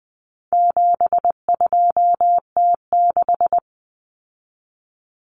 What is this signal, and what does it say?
Morse code, heard as a keyed tone, about 20 wpm, 700 Hz. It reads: 72T6
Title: Common Cut Numbers - 4-Digit 20wpm